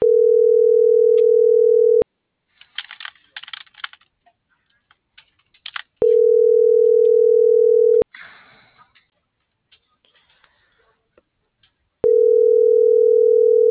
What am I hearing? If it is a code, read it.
Ringback tone (North America), 440 + 480 Hz.
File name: CENTRAL TERRESTRE / out vane.lazaro